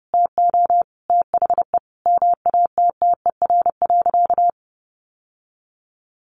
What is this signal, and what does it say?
Morse code, heard as a keyed tone, about 30 wpm, 700 Hz. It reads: TO THE MATTER.
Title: Sentences from Top 1200 Words 30wpm